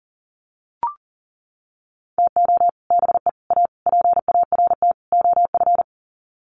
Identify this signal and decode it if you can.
Morse code, heard as a keyed tone, about 40 wpm, 700 Hz. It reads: TO BE A PART OF
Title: Sets of 5 Words 40wpm